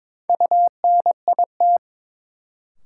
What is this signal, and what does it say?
Morse code, heard as a keyed tone, about 22 wpm, 700 Hz. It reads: UNIT
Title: Sentences from Top 1000 Words - Review Part 2 - New Word Drill - 22wpm